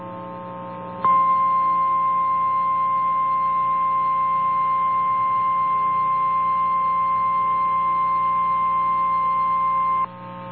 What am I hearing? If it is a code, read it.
Emergency Alert System (SAME) tones (the 1050 Hz attention tone). It is heard from 1.0 to 10.1 s.
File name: eas / KNCYKBIE / KNCYKBIE